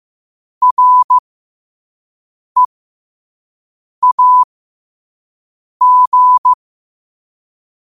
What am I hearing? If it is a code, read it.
Morse code, heard as a keyed tone, about 15 wpm, spaced as for 5 wpm, 1000 Hz. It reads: REAG
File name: qotdfiles / 5wpm